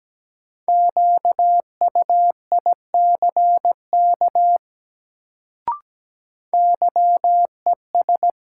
Morse code, heard as a keyed tone, about 17 wpm, 700 Hz. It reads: QUICK YES
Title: Sentences from Top 500 Words - New Word Drill - 17wpm